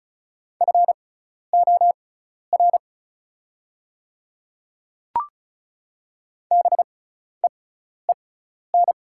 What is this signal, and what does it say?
Morse code, heard as a keyed tone, about 36 wpm, spaced as for 12 wpm, 700 Hz. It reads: FOR BEEN
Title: 6x Character Spacing - Top 100 Words - 35wpm